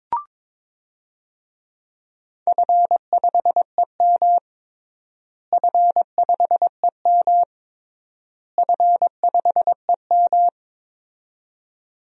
Morse code, heard as a keyed tone, about 22 wpm, 700 Hz. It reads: F5EM F5EM F5EM
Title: DXCC-100 Call Signs - 4 Characters - Repeated 3x - 22wpm